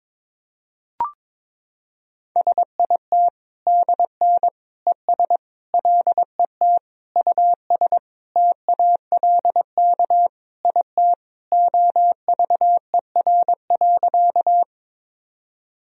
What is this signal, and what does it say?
Morse code, heard as a keyed tone, about 22 wpm, 700 Hz. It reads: SIT DN ES LET US TALK IT OVER.
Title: Sentences from Top 500 Words and Abbreviations 22wpm